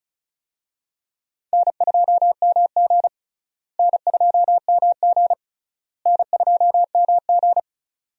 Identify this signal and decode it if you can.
Morse code, heard as a keyed tone, about 36 wpm, 700 Hz. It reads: N2MG N2MG N2MG
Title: USA Call Signs - 4 Characters - Repeated 3x - 35wpm